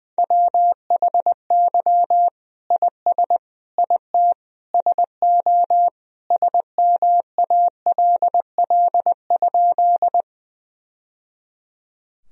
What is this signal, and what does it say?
Morse code, heard as a keyed tone, about 20 wpm, 700 Hz. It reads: WHY IS IT SO SMALL?
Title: Sentences from Top 1000 Words - Review Part 1 - 20wpm